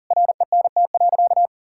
Morse code, heard as a keyed tone, about 37 wpm, 700 Hz. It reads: RENT.
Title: Sentences from Top 1000 Words - Review Part 5 - 40wpm